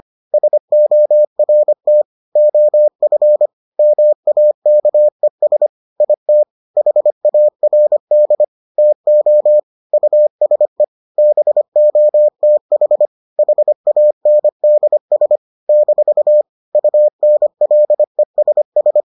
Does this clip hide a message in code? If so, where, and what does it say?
Yes: Morse code from 0.3 to 19.0 s, heard as a keyed tone, about 25 wpm, 600 Hz. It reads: SORT OF MAKES IT HARD TO USE BOTH HANDS - UNLESS